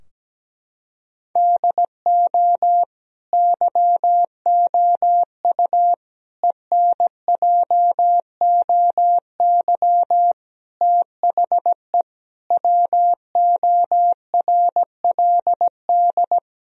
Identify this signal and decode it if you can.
Morse code, heard as a keyed tone, about 17 wpm, 700 Hz. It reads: DO YOU ENJOY THE WORLD
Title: Sentences from Top 1000 Words - Review Part 4 - 17wpm